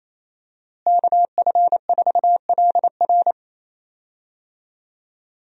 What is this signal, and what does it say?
Morse code, heard as a keyed tone, about 28 wpm, 700 Hz. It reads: KF4LR
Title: USA Call Signs - 5 Characters - Repeated 3x - 28wpm